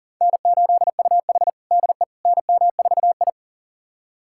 Morse code, heard as a keyed tone, about 40 wpm, 700 Hz. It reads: N8UH DE NM4I
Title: USA Callsign DE Callsign - 4 Characters - 40wpm